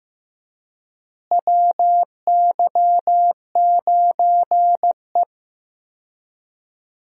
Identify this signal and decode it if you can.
Morse code, heard as a keyed tone, about 15 wpm, 700 Hz. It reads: WY9E
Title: USA Call Signs - 4 Characters - Repeated 2x - 15wpm